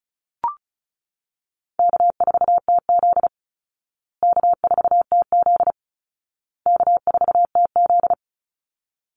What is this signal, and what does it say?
Morse code, heard as a keyed tone, about 34 wpm, 700 Hz. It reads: K4TZ K4TZ K4TZ
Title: Easy Call Signs Repeated 3x - 35wpm